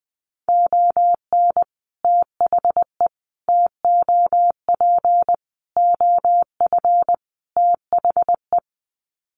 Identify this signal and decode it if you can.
Morse code, heard as a keyed tone, about 20 wpm, 700 Hz. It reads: ON THE TOP OF THE
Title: Sets of 5 Words Encore 20wpm